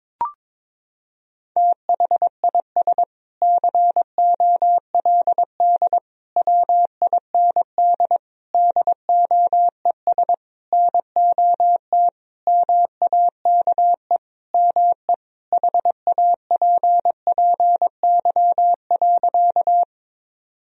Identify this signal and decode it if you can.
Morse code, heard as a keyed tone, about 22 wpm, 700 Hz. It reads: THIS COLD WIND DOES NOT MAKE ME HAPPY.
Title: Sentences from Top 1000 Words - Review Part 2 - 22wpm